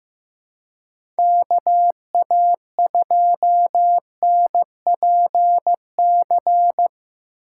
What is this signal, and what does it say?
Morse code, heard as a keyed tone, about 15 wpm, 700 Hz. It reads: KA2NPC
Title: USA Call Signs - 4x Character Spacing - 15wpm